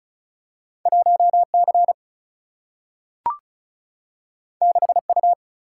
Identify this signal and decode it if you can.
Morse code, heard as a keyed tone, about 36 wpm, 700 Hz. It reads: 1C 6U
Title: Single-Letter Single-Number 35wpm